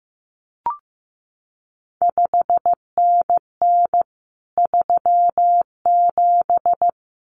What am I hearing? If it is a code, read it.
Morse code, heard as a keyed tone, about 15 wpm, 700 Hz. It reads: 5NN 37
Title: CQ WW DX Contest Exchange Encore - 15wpm